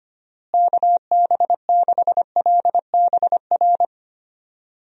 Morse code, heard as a keyed tone, about 25 wpm, 700 Hz. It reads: KB6LBR
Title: USA Call Signs - 6 Characters - Repeated 2x - 25wpm